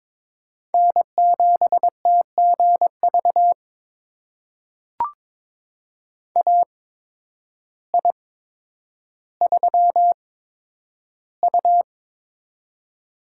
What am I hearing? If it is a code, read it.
Morse code, heard as a keyed tone, about 22 wpm, spaced as for 10 wpm, 700 Hz. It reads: N7TGV AI3U
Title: USA Call Signs - 8x Character Spacing - 22wpm